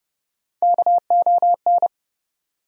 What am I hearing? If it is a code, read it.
Morse code, heard as a keyed tone, about 30 wpm, 700 Hz. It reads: KON
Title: USA Call Sign Postfixes - 3 Letters 30wpm